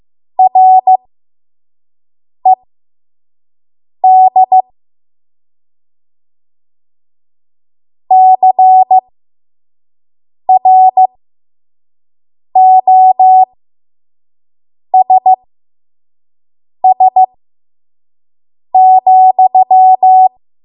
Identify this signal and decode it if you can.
Morse code, heard as a keyed tone, about 15 wpm, spaced as for 5 wpm, 750 Hz. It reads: RED CROSS,
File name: Morse / 05WPM